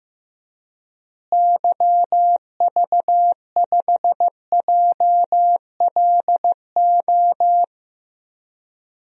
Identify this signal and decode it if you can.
Morse code, heard as a keyed tone, about 15 wpm, 700 Hz. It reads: YV5JLO
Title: DXCC-100 Call Signs - 6 Characters - Repeated 3x - 15wpm